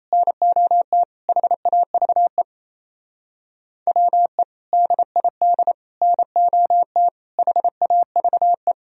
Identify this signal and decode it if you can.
Morse code, heard as a keyed tone, about 34 wpm, 700 Hz. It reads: NOT HAVE WE DID NOT HAVE
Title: Speed-Racing - Sets of 4 Words 28wpm